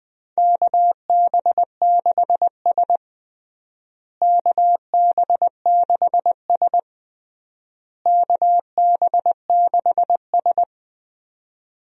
Morse code, heard as a keyed tone, about 20 wpm, 700 Hz. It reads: KB6S KB6S KB6S
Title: USA Call Signs - 4 Characters - Repeated 3x - 20wpm